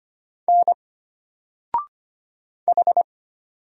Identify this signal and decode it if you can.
Morse code, heard as a keyed tone, about 25 wpm, 700 Hz. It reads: N H
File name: Single-Letter-Number-25wpm